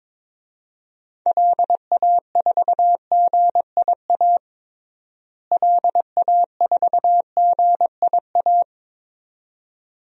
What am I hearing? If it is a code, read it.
Morse code, heard as a keyed tone, about 22 wpm, 700 Hz. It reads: LA4GIA LA4GIA
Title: DXCC-100 Call Signs - 6 Characters - Repeated 2x - 22wpm